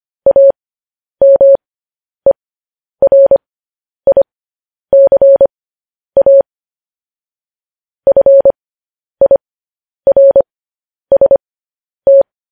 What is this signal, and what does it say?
Morse code, heard as a keyed tone, about 25 wpm, spaced as for 10 wpm, 550 Hz. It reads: AMERICA FIRST
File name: podcast / News headlines-1x-25x10